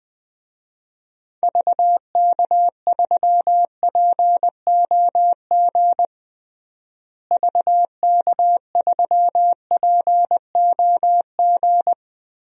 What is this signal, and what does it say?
Morse code, heard as a keyed tone, about 20 wpm, 700 Hz. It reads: VK3POG VK3POG
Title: DXCC-100 Call Signs - 6 Characters - Repeated 2x - 20wpm